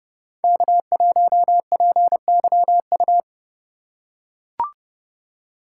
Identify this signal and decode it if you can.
Morse code, heard as a keyed tone, about 29 wpm, 700 Hz. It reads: K1PYU